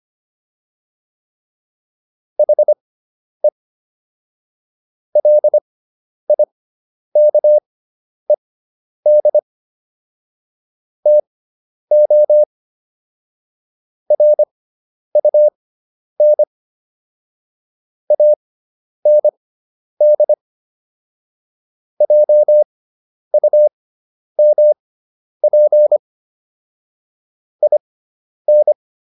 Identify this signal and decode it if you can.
Morse code, heard as a keyed tone, about 25 wpm, spaced as for 10 wpm, 600 Hz. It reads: HE LIKED TO RUN AND JUMP IN